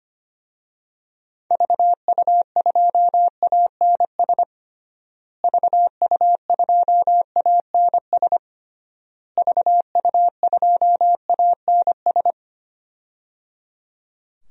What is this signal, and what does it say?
Morse code, heard as a keyed tone, about 25 wpm, 700 Hz. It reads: VU2ANS VU2ANS VU2ANS
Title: DXCC-100 Call Signs - 6 Characters - Repeated 3x - 25wpm